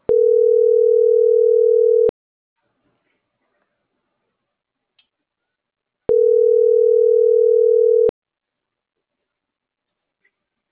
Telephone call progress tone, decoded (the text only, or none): ringback tone